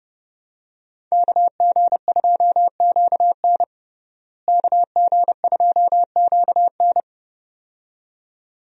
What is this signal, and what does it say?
Morse code, heard as a keyed tone, about 30 wpm, 700 Hz. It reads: KG2QN KG2QN
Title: USA Call Signs - 5 Characters - Repeated 2x - 30wpm